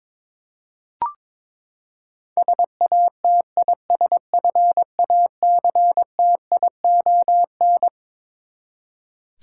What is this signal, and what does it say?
Morse code, heard as a keyed tone, about 22 wpm, 700 Hz. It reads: SATISFACTION
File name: Twelve Letter Words - 22wpm